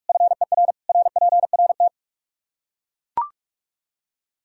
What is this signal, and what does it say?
Morse code, heard as a keyed tone, about 43 wpm, 700 Hz. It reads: FER RPRT
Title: Practice QSO - Exchange 2 - Info - 45wpm